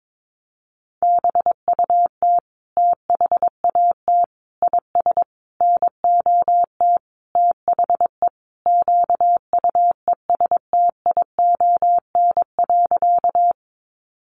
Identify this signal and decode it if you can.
Morse code, heard as a keyed tone, about 22 wpm, 700 Hz. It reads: BUT THAT IS NOT THE QUESTION.
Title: Sentences from Top 600 Words 22wpm